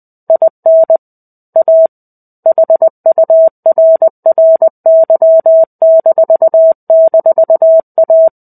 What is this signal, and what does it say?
Morse code, heard as a keyed tone, about 20 wpm, 650 Hz. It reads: IN A HURRY--A
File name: 05-Chapter-05 20 WPM